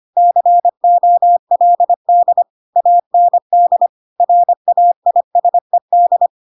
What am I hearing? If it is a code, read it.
Morse code, heard as a keyed tone, about 25 wpm, 700 Hz. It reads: COLD AND RAISED
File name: Pinocchio / Pinocchio ch6 25 25 ik1hge